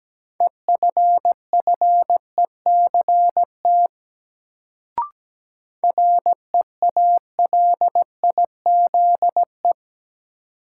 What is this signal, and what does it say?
Morse code, heard as a keyed tone, about 17 wpm, 700 Hz. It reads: EFFECT REALIZE
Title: Sentences from Top 1000 Words - New Word Drill - 17wpm